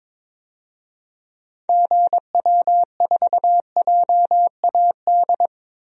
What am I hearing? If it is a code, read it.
Morse code, heard as a keyed tone, about 23 wpm, 700 Hz. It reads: GW4JAD
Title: DXCC-100 Call Signs - 6 Characters - 22wpm